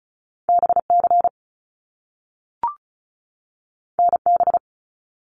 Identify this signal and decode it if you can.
Morse code, heard as a keyed tone, about 36 wpm, 700 Hz. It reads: BC NB
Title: Canadian Provinces and Territories 35wpm